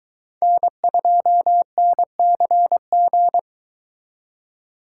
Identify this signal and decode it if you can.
Morse code, heard as a keyed tone, about 22 wpm, 700 Hz. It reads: N2NCG